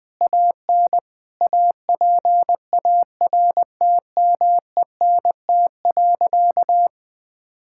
Morse code, heard as a keyed tone, about 20 wpm, 700 Hz. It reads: AN APARTMENT.
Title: Sentences from Top 1000 Words - Review Part 6 - 20wpm